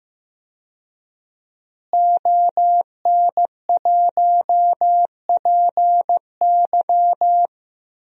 Morse code, heard as a keyed tone, about 15 wpm, 700 Hz. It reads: ON1PY
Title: DXCC-100 Call Signs - 5 Characters - Repeated 2x - 15wpm